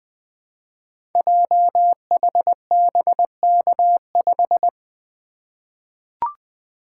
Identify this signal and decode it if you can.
Morse code, heard as a keyed tone, about 20 wpm, 700 Hz. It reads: JHBK5
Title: Code Groups - 5 Letter:Numbers - 20wpm